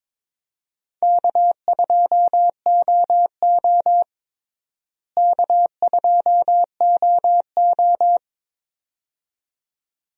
Morse code, heard as a keyed tone, about 22 wpm, 700 Hz. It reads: K2OO K2OO
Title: USA Call Signs - 4 Characters - Repeated 2x - 22wpm